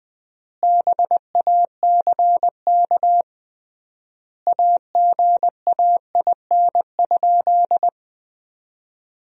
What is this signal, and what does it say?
Morse code, heard as a keyed tone, about 20 wpm, 700 Hz. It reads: BACK AGAIN?
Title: 3x Word-Spacing - Sentences from Top 200 Words - 20wpm